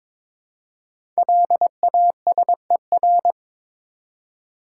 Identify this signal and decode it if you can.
Morse code, heard as a keyed tone, about 23 wpm, 700 Hz. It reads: LASER
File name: 5 Letter Words - First 3 Letters Given - 22wpm